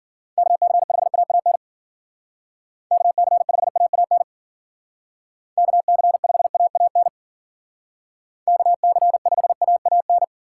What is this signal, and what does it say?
Morse code, heard as a keyed tone, about 49 wpm, 700 Hz. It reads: KC5AAN KC5AAN KC5AAN KC5AAN
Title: Speed Racing - USA 2x3 Call Signs 40wpm